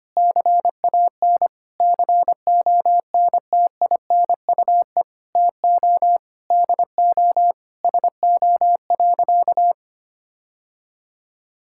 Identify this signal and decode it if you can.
Morse code, heard as a keyed tone, about 25 wpm, 700 Hz. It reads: CAN CONTINUE TO DO SO.
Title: Sentences from Top 1000 Words - Review Part 6 - 25wpm